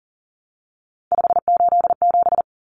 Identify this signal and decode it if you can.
Morse code, heard as a keyed tone, about 40 wpm, 700 Hz. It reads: 587
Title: Numbers - 3-Digit 40wpm